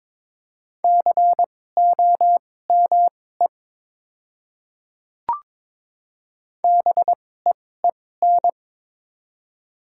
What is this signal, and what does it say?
Morse code, heard as a keyed tone, about 21 wpm, spaced as for 16 wpm, 700 Hz. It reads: COME BEEN
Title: 2x Character Spacing - Top 100 Words - 22wpm